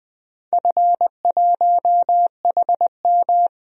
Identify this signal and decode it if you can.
Morse code, heard as a keyed tone, about 20 wpm, 700 Hz. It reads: F1HM
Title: DXCC-100 Call Signs - 4 Characters - Repeated 2x - 20wpm